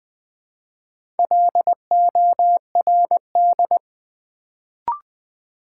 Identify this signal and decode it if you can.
Morse code, heard as a keyed tone, about 20 wpm, 700 Hz. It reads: LORD
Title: Build A Word - 1 Letter At A Time - 4-Letter Words - 20wpm